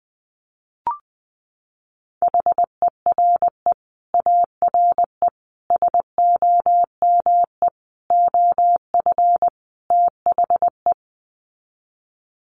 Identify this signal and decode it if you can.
Morse code, heard as a keyed tone, about 20 wpm, 700 Hz. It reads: HERE ARE SOME OF THE
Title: Sets of 5 Words Encore 20wpm